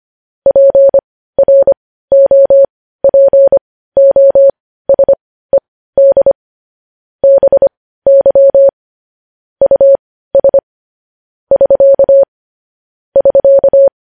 Morse code, heard as a keyed tone, about 25 wpm, spaced as for 15 wpm, 550 Hz. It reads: PROPOSED BY US <SK> <SK>
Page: News Headlines in Morse Code at 15 WPM